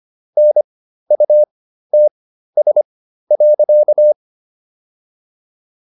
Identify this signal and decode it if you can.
Morse code, heard as a keyed tone, about 25 wpm, spaced as for 13 wpm, 600 Hz. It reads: NUTS.